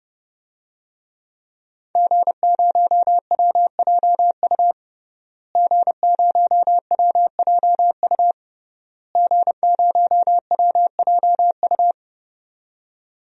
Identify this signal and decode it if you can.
Morse code, heard as a keyed tone, about 32 wpm, 700 Hz. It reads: G0WJU G0WJU G0WJU
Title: DXCC-100 Call Signs - 5 Characters - Repeated 3x - 30wpm